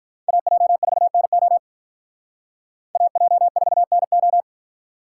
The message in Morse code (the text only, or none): AJ4NO AJ4NO